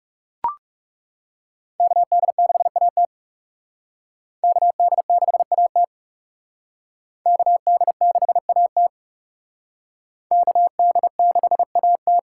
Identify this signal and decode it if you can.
Morse code, heard as a keyed tone, about 37 wpm, 700 Hz. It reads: KD6AT KD6AT KD6AT KD6AT
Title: Speed Racing - USA 2x2 Call Signs 30wpm